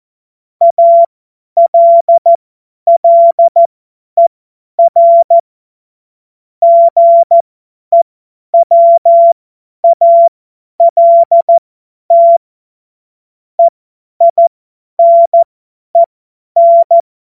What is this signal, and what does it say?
Morse code, heard as a keyed tone, about 14 wpm, spaced as for 10 wpm, 680 Hz. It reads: ALLER GEWALT EINEN
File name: Morsekurs / Witze / Witz 12. April 2022 50 F70